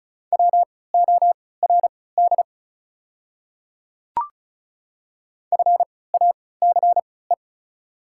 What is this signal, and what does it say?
Morse code, heard as a keyed tone, about 36 wpm, spaced as for 20 wpm, 690 Hz. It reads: WORD FACE
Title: 3x Character Spacing - Top 300 Words - 35wpm